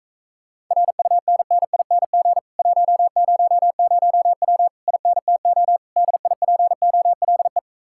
Morse code, heard as a keyed tone, about 44 wpm, 700 Hz. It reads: RUNNING 100W INTO DIPOLE